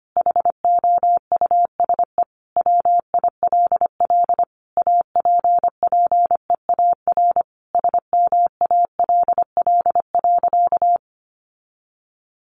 Morse code, heard as a keyed tone, about 25 wpm, 700 Hz. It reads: HOUSE WILL APPEAR SMALL.
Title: Sentences from Top 1500 Words 25wpm